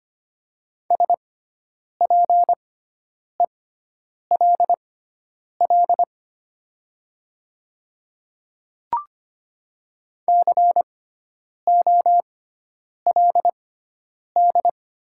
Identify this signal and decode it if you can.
Morse code, heard as a keyed tone, about 25 wpm, spaced as for 9 wpm, 700 Hz. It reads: SPELL COLD
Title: 6x Character Spacing - Top 500 Words - 25wpm